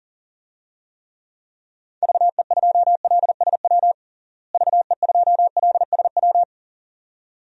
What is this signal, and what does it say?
Morse code, heard as a keyed tone, about 40 wpm, 700 Hz. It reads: VE2LSW VE2LSW
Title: DXCC-100 Call Signs - 6 Characters - Repeated 2x - 40wpm